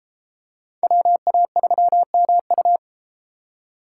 Morse code, heard as a keyed tone, about 30 wpm, 700 Hz. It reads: WA3MU